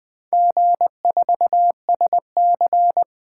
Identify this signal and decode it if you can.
Morse code, heard as a keyed tone, about 20 wpm, 700 Hz. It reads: G4SC
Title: Four Letter-Number Combinations - 20wpm